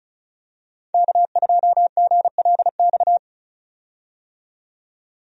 Morse code, heard as a keyed tone, about 36 wpm, 700 Hz. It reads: K2GLX
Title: USA Call Signs - 5 Characters - 35wpm